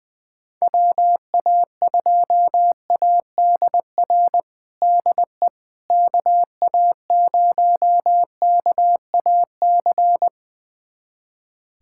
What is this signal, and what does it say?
Morse code, heard as a keyed tone, about 21 wpm, 700 Hz. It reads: WA2ADR DE KA0KAC
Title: USA Callsign DE Callsign - 6 Characters - 20wpm